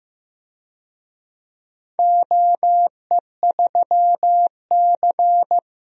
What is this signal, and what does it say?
Morse code, heard as a keyed tone, about 15 wpm, 700 Hz. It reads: OE3C
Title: DXCC-100 Call Signs - 4 Characters - Repeated 2x - 15wpm